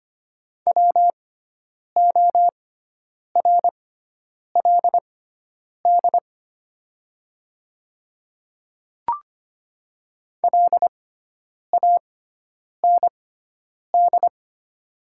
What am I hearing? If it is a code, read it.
Morse code, heard as a keyed tone, about 25 wpm, spaced as for 9 wpm, 700 Hz. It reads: WORLD LAND
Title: 6x Character Spacing - Top 200 Words - 25wpm